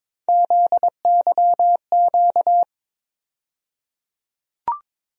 Morse code, heard as a keyed tone, about 22 wpm, 700 Hz. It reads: ZYQ
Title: USA Call Sign Postfixes - 3 Letters 22wpm